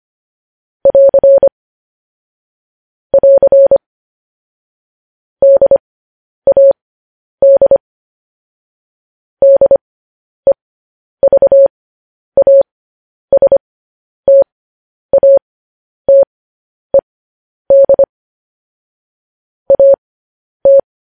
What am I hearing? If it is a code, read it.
Morse code, heard as a keyed tone, about 25 wpm, spaced as for 10 wpm, 550 Hz. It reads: <AR> <AR> DAD DEVASTATED AT
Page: news headlines in morse code at 10 wpm